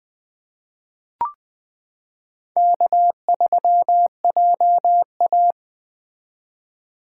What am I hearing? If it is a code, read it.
Morse code, heard as a keyed tone, about 21 wpm, 700 Hz. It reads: K3JA
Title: 2023 POTA Activator Call Signs - 20wpm